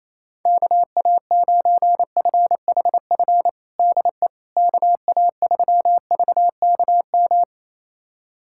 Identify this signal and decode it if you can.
Morse code, heard as a keyed tone, about 28 wpm, 700 Hz. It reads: KA9FHF DE KA3VKM
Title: USA Callsign DE Callsign - 6 Characters - 28wpm